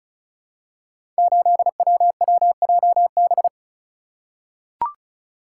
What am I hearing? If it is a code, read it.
Morse code, heard as a keyed tone, about 34 wpm, 700 Hz. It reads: 8WWJB
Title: Code Groups - 5 Letter:Numbers - 35wpm